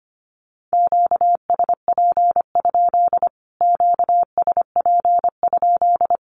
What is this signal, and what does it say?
Morse code, heard as a keyed tone, about 25 wpm, 700 Hz. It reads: QSP? QSP?
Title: Q-Codes Repeated 3x 25wpm